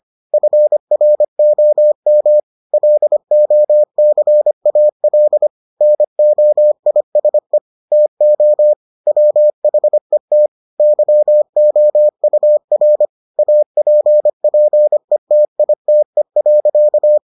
Morse code, heard as a keyed tone, about 25 wpm, 600 Hz. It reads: FROM LOCAL NOISE TO WHET YOUR APPETITE.